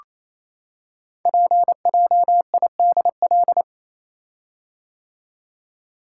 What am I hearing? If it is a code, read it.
Morse code, heard as a keyed tone, about 27 wpm, 700 Hz. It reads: PJIDL
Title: Code Groups - 5 Letters - 28wpm